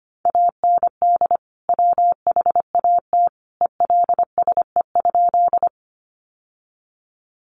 Morse code, heard as a keyed tone, about 25 wpm, 700 Hz. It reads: AND WHAT ELSE?
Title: Sentences from Top 800 Words 25wpm